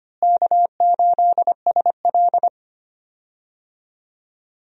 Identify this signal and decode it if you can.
Morse code, heard as a keyed tone, about 25 wpm, 700 Hz. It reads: K8SL
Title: USA Call Signs - 4 Characters - Repeated 2x - 25wpm